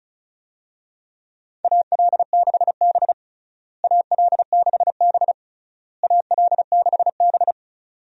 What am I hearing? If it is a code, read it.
Morse code, heard as a keyed tone, about 36 wpm, 700 Hz. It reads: AL6B AL6B AL6B
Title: USA Call Signs - 4 Characters - Repeated 3x - 35wpm